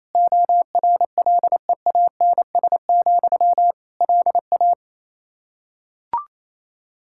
Morse code, heard as a keyed tone, about 28 wpm, 700 Hz. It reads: ORLEANS, LA
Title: Practice QSO - Exchange 1 - Beginning - 28wpm